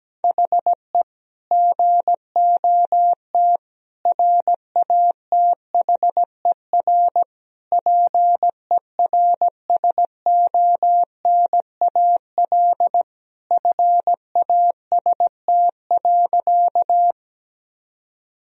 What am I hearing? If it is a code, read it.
Morse code, heard as a keyed tone, about 17 wpm, 700 Hz. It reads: HE GOT RATHER PERSONAL FAST.
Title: Sentences from Top 1000 Words - Review Part 4 - 17wpm